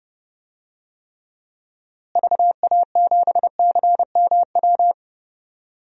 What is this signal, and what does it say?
Morse code, heard as a keyed tone, about 31 wpm, 700 Hz. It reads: VA7CMW
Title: DXCC-100 Call Signs - 6 Characters - Repeated 3x - 30wpm